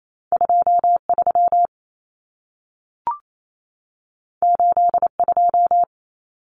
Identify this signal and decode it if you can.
Morse code, heard as a keyed tone, about 28 wpm, 700 Hz. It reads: 23 82
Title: Numbers - 2-Digit 28wpm